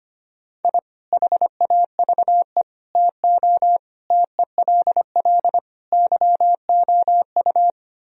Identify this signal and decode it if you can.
Morse code, heard as a keyed tone, about 25 wpm, 700 Hz. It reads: I HAVE TO TELL YOU
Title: Speed-Racing - Sets of 5 Words 25wpm